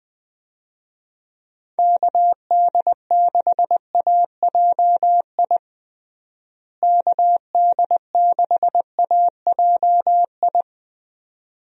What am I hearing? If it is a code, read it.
Morse code, heard as a keyed tone, about 21 wpm, 700 Hz. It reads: KD6AJI KD6AJI
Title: USA Call Signs - 6 Characters - Repeated 2x - 20wpm